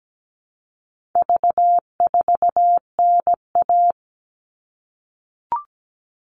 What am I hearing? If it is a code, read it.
Morse code, heard as a keyed tone, about 17 wpm, 700 Hz. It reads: V4NA
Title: All Cut Numbers - 4-Digit 17wpm